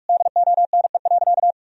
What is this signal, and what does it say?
Morse code, heard as a keyed tone, about 41 wpm, 700 Hz. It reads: DONE.
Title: Sentences from Top 1000 Words - Review Part 1 - 45wpm